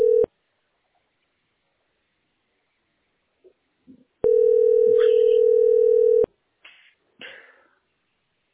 Ringback tone (North America), 440 + 480 Hz.